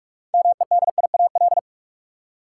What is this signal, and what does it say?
Morse code, heard as a keyed tone, about 47 wpm, 700 Hz. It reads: MEDIAL
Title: Practice QSO - Exchange 3 - More Info - 45wpm